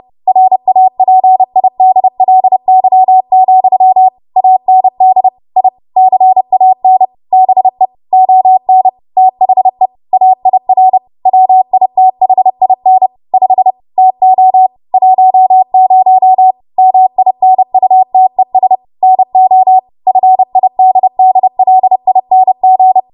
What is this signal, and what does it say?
Morse code, heard as a keyed tone, about 30 wpm, 750 Hz. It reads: RAPIDLY, AND I CAN BE ON THE AIR WITHIN 5 TO 10 MINUTES NO FIDDLING
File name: Morse / 30WPM